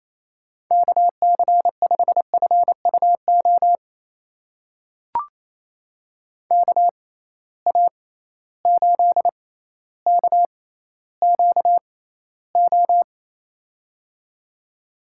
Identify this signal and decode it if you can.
Morse code, heard as a keyed tone, about 28 wpm, spaced as for 14 wpm, 700 Hz. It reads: KC5FUO KA8KQO
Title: USA Call Signs - 6 Characters - 6x Character Spacing - 28wpm